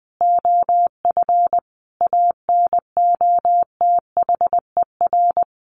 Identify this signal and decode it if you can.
Morse code, heard as a keyed tone, about 20 wpm, 700 Hz. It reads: OF ANOTHER
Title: Sentences from Top 1400 Words 20wpm